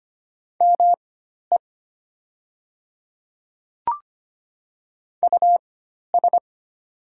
Morse code, heard as a keyed tone, about 25 wpm, spaced as for 12 wpm, 700 Hz. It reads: ME US
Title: 4x Character Spacing - Top 200 Words - 25wpm